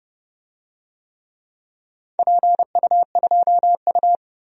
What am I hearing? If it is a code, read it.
Morse code, heard as a keyed tone, about 30 wpm, 700 Hz. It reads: PU2U
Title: DXCC-100 Call Signs - 4 Characters - Repeated 3x - 30wpm